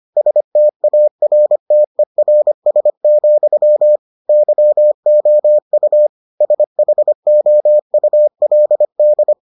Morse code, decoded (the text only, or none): STARTERS, YOU SHOULD